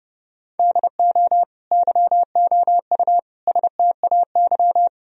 Morse code, heard as a keyed tone, about 32 wpm, 700 Hz. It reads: DO YOU STAY
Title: Sentences from Top 1000 Words - Review Part 4 - 30wpm